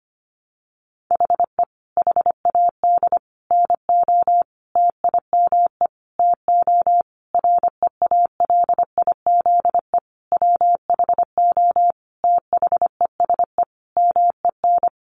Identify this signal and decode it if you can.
Morse code, heard as a keyed tone, about 25 wpm, 700 Hz. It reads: HE HAD NO TIME TO REALIZE WHO THESE MEN
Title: Sentences from Top 1000 Words 25wpm